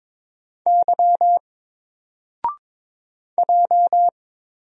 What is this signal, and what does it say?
Morse code, heard as a keyed tone, about 22 wpm, 700 Hz. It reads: Y J
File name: Single-Letter-Number-22wpm